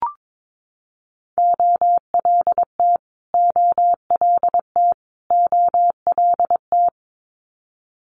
Morse code, heard as a keyed tone, about 22 wpm, 700 Hz. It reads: OLT OLT OLT
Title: Multi-letter Phonemes Repeated 3x 22wpm